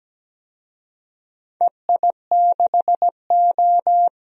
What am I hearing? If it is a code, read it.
Morse code, heard as a keyed tone, about 17 wpm, 700 Hz. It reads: EI6O
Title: DXCC-100 Call Signs - 4 Characters - 17wpm